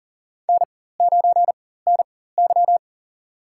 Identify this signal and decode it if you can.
Morse code, heard as a keyed tone, about 40 wpm, spaced as for 19 wpm, 700 Hz. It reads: N9NY
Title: USA Call Signs - 4 Characters - 4x Character Spacing - 40wpm